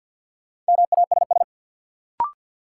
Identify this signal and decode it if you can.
Morse code, heard as a keyed tone, about 50 wpm, 700 Hz. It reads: MASS
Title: Sets of 5 Words Encore - New Word Drill - 50wpm